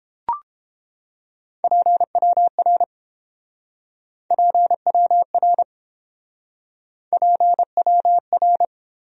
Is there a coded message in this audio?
Yes: Morse code from 1.6 to 8.7 s, heard as a keyed tone, about 29 wpm, 700 Hz. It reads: PWR PWR PWR